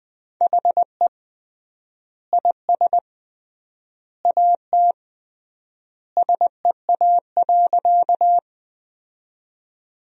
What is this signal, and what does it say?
Morse code, heard as a keyed tone, about 19 wpm, 700 Hz. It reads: HE IS AT SEA.
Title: 3x Word-Spacing - Sentences from Top 300 Words - 20wpm